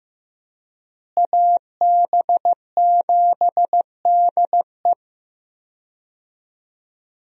Morse code, heard as a keyed tone, about 16 wpm, 700 Hz. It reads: AB7DE
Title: USA Call Signs - 5 Characters - Repeated 3x - 15wpm